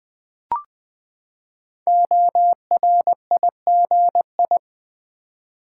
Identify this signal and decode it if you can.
Morse code, heard as a keyed tone, about 20 wpm, 700 Hz. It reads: ORIGI
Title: Build A Word - 5 Letters At A Time - 10-Letter Words - 20wpm